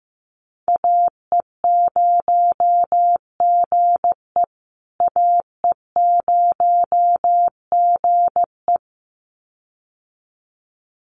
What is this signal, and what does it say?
Morse code, heard as a keyed tone, about 15 wpm, 700 Hz. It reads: AE0GE AE0GE
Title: USA Call Signs Repeated 2x 15wpm